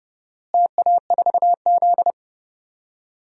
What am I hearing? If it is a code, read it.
Morse code, heard as a keyed tone, about 32 wpm, 700 Hz. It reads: TA4Z
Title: DXCC-100 Call Signs - 4 Characters - Repeated 2x - 30wpm